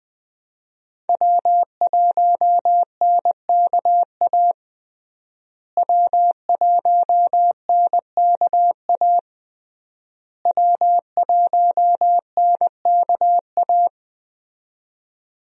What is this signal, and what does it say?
Morse code, heard as a keyed tone, about 20 wpm, 700 Hz. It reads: W1NKA W1NKA W1NKA
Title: USA Call Signs - 5 Characters - Repeated 3x - 20wpm